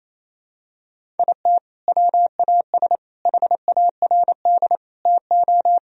Morse code, heard as a keyed tone, about 28 wpm, 700 Hz. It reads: IT WAS HARD TO
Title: Sentences from Top 1000 Words - Review Part 1 - 28wpm